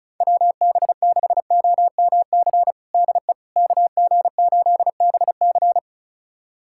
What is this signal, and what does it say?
Morse code, heard as a keyed tone, about 34 wpm, 690 Hz. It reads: WB6OMC DE KG8BC
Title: USA Callsign DE Callsign 35wpm